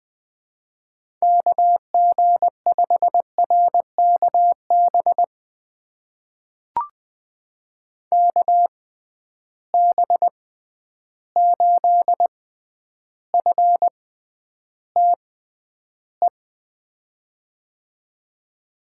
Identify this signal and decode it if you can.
Morse code, heard as a keyed tone, about 20 wpm, spaced as for 10 wpm, 700 Hz. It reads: KG5RKB KB8FTE
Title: USA Call Signs - 6 Characters - 6x Character Spacing - 20wpm